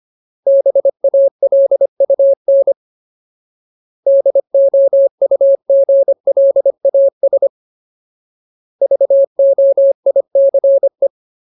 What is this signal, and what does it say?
Morse code, heard as a keyed tone, about 25 wpm, 550 Hz. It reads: BALUN DOUGLAS VOICE
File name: Ultimate Word List / 25WPM Ultimate